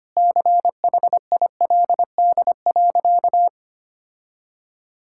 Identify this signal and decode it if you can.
Morse code, heard as a keyed tone, about 25 wpm, 700 Hz. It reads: CHILD.
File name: practice / 25wpm / Sentences with 4-Word Phrases - 25wpm